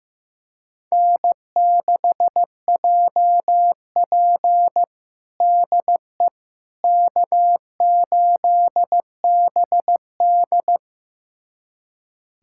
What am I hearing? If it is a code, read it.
Morse code, heard as a keyed tone, about 16 wpm, 700 Hz. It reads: N6JP DE K8BD
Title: USA Callsign DE Callsign - 4 Characters - 15wpm